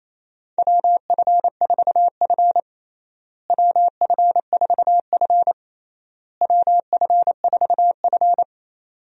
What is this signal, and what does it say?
Morse code, heard as a keyed tone, about 28 wpm, 700 Hz. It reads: WF4F WF4F WF4F
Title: USA Call Signs - 4 Characters - Repeated 3x - 28wpm